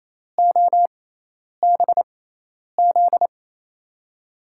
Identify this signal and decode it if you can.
Morse code, heard as a keyed tone, about 28 wpm, spaced as for 10 wpm, 700 Hz. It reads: OBZ
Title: USA Call Signs - 6 Characters - 6x Character Spacing - 28wpm